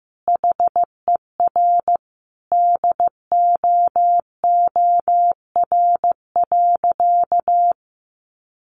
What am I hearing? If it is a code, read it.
Morse code, heard as a keyed tone, about 15 wpm, 700 Hz. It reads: HER DOOR.
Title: Sentences from Top 1400 Words 15wpm